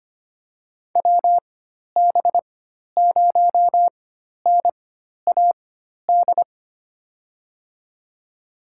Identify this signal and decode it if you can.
Morse code, heard as a keyed tone, about 25 wpm, spaced as for 12 wpm, 700 Hz. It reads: WB0NAD
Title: USA Call Signs - 6 Characters - 4x Character Spacing - 25wpm